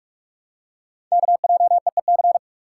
Morse code, heard as a keyed tone, about 44 wpm, 700 Hz. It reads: KJEEC
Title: Code Groups - 5 Letters - 45wpm